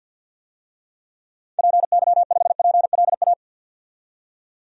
Morse code, heard as a keyed tone, about 52 wpm, 700 Hz. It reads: PY5PLA
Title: DXCC-100 Call Signs - 6 Characters - 50wpm